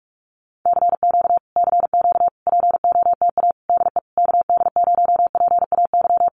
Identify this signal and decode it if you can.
Morse code, heard as a keyed tone, about 44 wpm, 700 Hz. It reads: CQ CQ POTA DE KD0PAY
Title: Common QSO Phrases Encore 45wpm